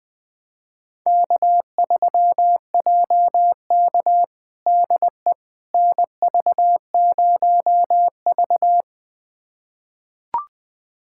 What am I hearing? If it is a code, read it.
Morse code, heard as a keyed tone, about 20 wpm, 700 Hz. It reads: K3JK DE NV0V
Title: USA Callsign DE Callsign - 4 Characters - 20wpm